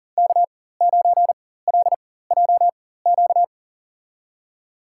Morse code, heard as a keyed tone, about 40 wpm, spaced as for 19 wpm, 700 Hz. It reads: K9LJQ
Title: USA Call Signs - 5 Characters - 4x Character Spacing - 40wpm